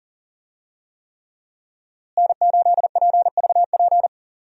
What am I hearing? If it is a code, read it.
Morse code, heard as a keyed tone, about 40 wpm, 700 Hz. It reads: N8PVP
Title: USA Call Signs - 5 Characters - Repeated 3x - 40wpm